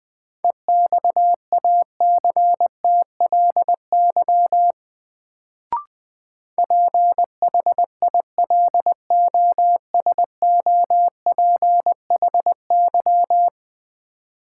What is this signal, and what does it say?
Morse code, heard as a keyed tone, about 20 wpm, 700 Hz. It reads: EXACTLY PHILOSOPHY
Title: Sentences from Top 900 Words - New Word Drill - 20wpm